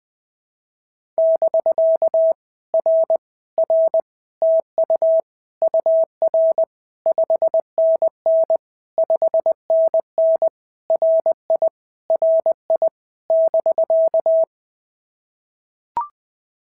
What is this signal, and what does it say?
Morse code, heard as a keyed tone, about 20 wpm, 650 Hz. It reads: <BK> R R TU UR 5NN 5NN RI RI <BK>
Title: POTA QSO Practice - 20wpm